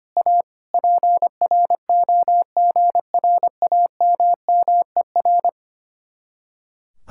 Morse code, heard as a keyed tone, about 29 wpm, 700 Hz. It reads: A PROGRAMMER
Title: Speed-Racing Common QSO Phrases Encore - 25wpm